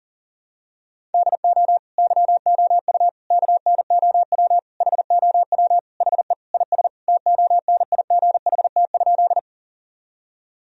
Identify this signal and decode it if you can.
Morse code, heard as a keyed tone, about 40 wpm, 700 Hz. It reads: DO YOU KNOW HOW HE IS TONIGHT?
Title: Sentences from Top 1000 Words - Review Part 3 - 40wpm